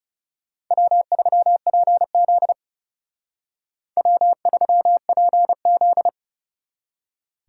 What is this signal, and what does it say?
Morse code, heard as a keyed tone, about 39 wpm, 700 Hz. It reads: W3PZ W3PZ